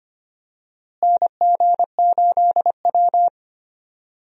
Morse code, heard as a keyed tone, about 25 wpm, 700 Hz. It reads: NG8W
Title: USA Call Signs - 4 Characters - Repeated 2x - 25wpm